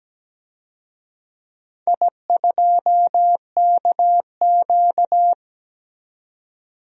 Morse code, heard as a keyed tone, about 17 wpm, 700 Hz. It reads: I2KQ
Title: DXCC-100 Call Signs - 4 Characters - 17wpm